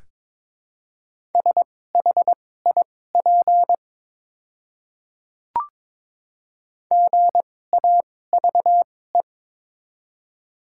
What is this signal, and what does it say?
Morse code, heard as a keyed tone, about 22 wpm, spaced as for 16 wpm, 690 Hz. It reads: SHIP GAVE
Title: 2x Character Spacing - Top 500 Words - 22wpm